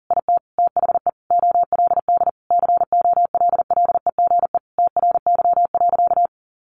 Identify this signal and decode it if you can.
Morse code, heard as a keyed tone, about 40 wpm, 700 Hz. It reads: IT THE OLD COLLEGE TRY.
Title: Sentences from Top 1300 Words 40wpm